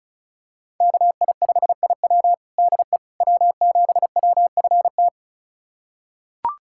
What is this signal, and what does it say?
Morse code, heard as a keyed tone, about 36 wpm, 700 Hz. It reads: KI5IW DE W7WFT
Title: USA Callsign DE Callsign 35wpm